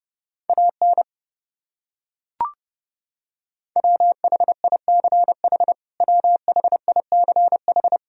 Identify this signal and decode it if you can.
Morse code, heard as a keyed tone, about 32 wpm, 700 Hz. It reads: AN WHICH WHICH
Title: Top 100 Words Repeated 30wpm